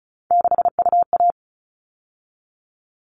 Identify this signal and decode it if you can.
Morse code, heard as a keyed tone, about 34 wpm, 700 Hz. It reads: 6UA
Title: All Cut Numbers - 3-Digit 35wpm